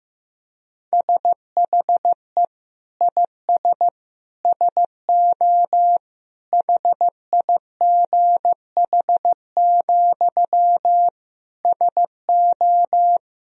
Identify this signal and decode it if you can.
Morse code, heard as a keyed tone, about 15 wpm, 700 Hz. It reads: SHE IS SO HIGH, SO
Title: Sentences from Top 1000 Words - Review Part 1 - 15wpm